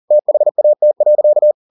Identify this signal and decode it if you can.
Morse code, heard as a keyed tone, about 40 wpm, 600 Hz. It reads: THAT.